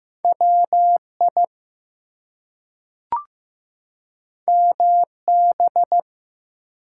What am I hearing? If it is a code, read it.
Morse code, heard as a keyed tone, about 15 wpm, 700 Hz. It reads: WI MB
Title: US States and Canadian Provinces 15wpm